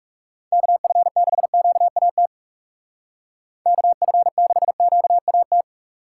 Morse code, heard as a keyed tone, about 36 wpm, 700 Hz. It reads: KF6QAT KF6QAT
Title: Speed Racing - USA 2x3 Call Signs 30wpm